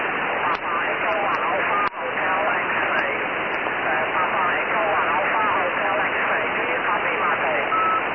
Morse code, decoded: none